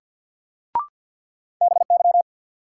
Morse code, heard as a keyed tone, about 53 wpm, 700 Hz. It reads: BY
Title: Sets of 2 Words - New Word Drill - 50wpm